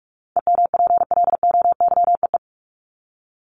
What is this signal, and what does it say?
Morse code, heard as a keyed tone, about 44 wpm, 700 Hz. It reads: EMPLOYEE
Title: Top 1501-1750 Words 45wpm